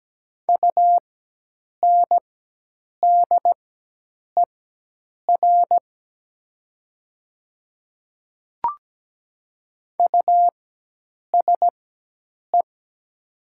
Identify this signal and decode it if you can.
Morse code, heard as a keyed tone, about 17 wpm, spaced as for 8 wpm, 700 Hz. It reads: UNDER USE
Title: 4x Character Spacing - Top 300 Words - 17wpm